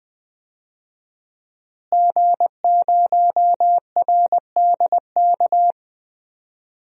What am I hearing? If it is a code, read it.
Morse code, heard as a keyed tone, about 20 wpm, 700 Hz. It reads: G0RDK
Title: DXCC-100 Call Signs - 5 Characters - Repeated 2x - 20wpm